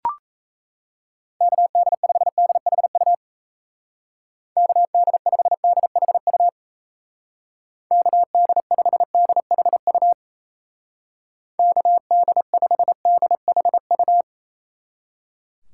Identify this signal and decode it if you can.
Morse code, heard as a keyed tone, about 35 wpm, 700 Hz. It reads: KD5DHU KD5DHU KD5DHU KD5DHU
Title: Speed Racing - USA Call Signs 28wpm